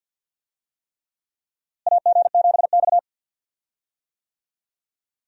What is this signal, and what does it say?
Morse code, heard as a keyed tone, about 51 wpm, 700 Hz. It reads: AG7X
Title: USA Call Signs - 4 Characters - Repeated 3x - 50wpm